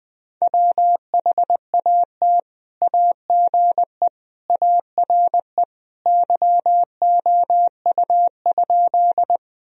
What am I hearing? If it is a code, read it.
Morse code, heard as a keyed tone, about 20 wpm, 700 Hz. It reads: WHAT AGE ARE YOU?
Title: Sentences from Top 1000 Words - Review Part 3 - 20wpm